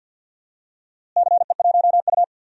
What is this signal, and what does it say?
Morse code, heard as a keyed tone, about 51 wpm, 700 Hz. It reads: CE1U
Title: DXCC-100 Call Signs - 4 Characters - Repeated 3x - 50wpm